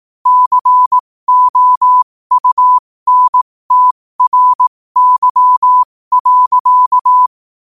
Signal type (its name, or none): Morse code